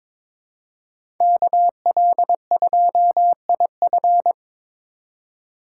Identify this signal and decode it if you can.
Morse code, heard as a keyed tone, about 22 wpm, 700 Hz. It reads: KL2IF
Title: USA Call Signs - 5 Characters - 3x Character Spacing - 22wpm